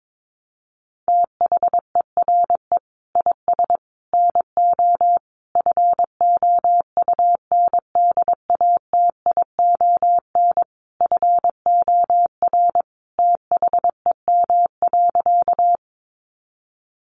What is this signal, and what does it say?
Morse code, heard as a keyed tone, about 22 wpm, 700 Hz. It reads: THERE IS NO FOUNDATION FOR THEM.
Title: Sentences from Top 1200 Words 22wpm